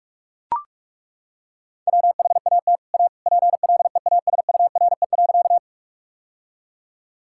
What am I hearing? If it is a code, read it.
Morse code, heard as a keyed tone, about 41 wpm, 700 Hz. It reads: WHAT A PLEASURE.
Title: Sentences from Top 1600 Words 45wpm